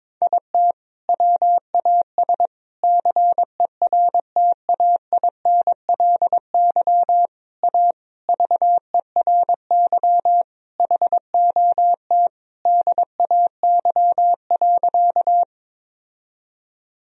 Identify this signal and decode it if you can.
Morse code, heard as a keyed tone, about 22 wpm, 700 Hz. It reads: IT WAS CERTAINLY A VERY HOT DAY.
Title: Sentences from Top 1000 Words - Review Part 3 - 22wpm